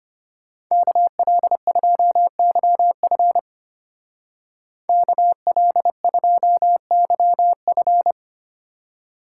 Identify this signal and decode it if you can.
Morse code, heard as a keyed tone, about 30 wpm, 700 Hz. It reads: KL2YF KL2YF
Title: Speed Racing - USA 2x2 Call Signs 25wpm